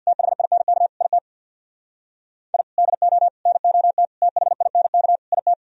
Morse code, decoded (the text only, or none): THINK IT I DO NOT THINK IT